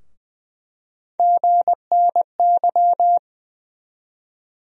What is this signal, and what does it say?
Morse code, heard as a keyed tone, about 20 wpm, 700 Hz. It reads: GNY
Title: USA Call Sign Postfixes - 3 Letters 20wpm